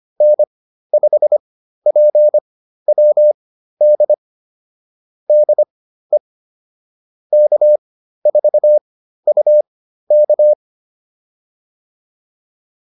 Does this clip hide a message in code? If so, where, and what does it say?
Yes: Morse code from 0.2 to 10.5 s, heard as a keyed tone, about 25 wpm, spaced as for 13 wpm, 600 Hz. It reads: N5PWD DE K4UK